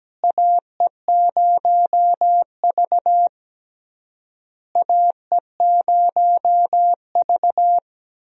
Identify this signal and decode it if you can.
Morse code, heard as a keyed tone, about 17 wpm, 700 Hz. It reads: AE0V AE0V
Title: USA Call Signs - 4 Characters - Repeated 3x - 17wpm